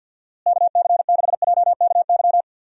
Morse code, heard as a keyed tone, about 51 wpm, 700 Hz. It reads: KC6JKY
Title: USA Call Signs - 6 Characters - Repeated 2x - 50wpm